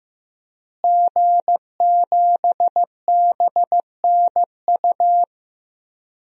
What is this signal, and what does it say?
Morse code, heard as a keyed tone, about 15 wpm, 700 Hz. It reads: G7BNU
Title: DXCC-100 Call Signs - 5 Characters - Repeated 3x - 15wpm